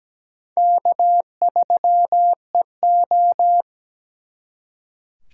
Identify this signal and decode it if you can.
Morse code, heard as a keyed tone, about 20 wpm, 700 Hz. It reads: K3EO
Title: Speed Racing - USA 1x2 Call Signs 17wpm